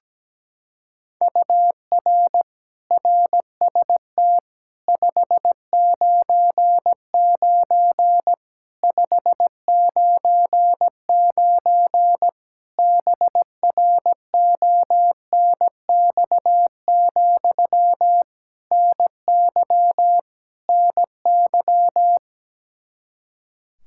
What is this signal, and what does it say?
Morse code, heard as a keyed tone, about 17 wpm, 700 Hz. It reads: UR RST 599 599 BRONX, NY NY
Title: Practice QSO - Exchange 1 - Beginning - 17wpm